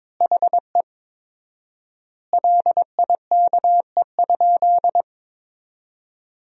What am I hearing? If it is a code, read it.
Morse code, heard as a keyed tone, about 22 wpm, 700 Hz. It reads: HE LIKE?
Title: 4x Word-Spacing - Sentences from Top 100 Words - 22wpm